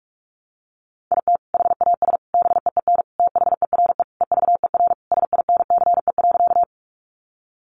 Morse code, heard as a keyed tone, about 46 wpm, 700 Hz. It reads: IT HAS BEEN THERE EVER SINCE.
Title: Sentences from Top 400 Words 45wpm